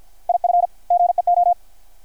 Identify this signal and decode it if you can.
Morse code, heard as a keyed tone, about 53 wpm, 720 Hz. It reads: IF GEO
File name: archived / CRSnet